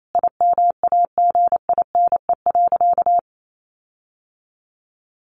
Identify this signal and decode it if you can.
Morse code, heard as a keyed tone, about 28 wpm, 700 Hz. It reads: IMAGINE.
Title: Sentences from Top 1400 Words 28wpm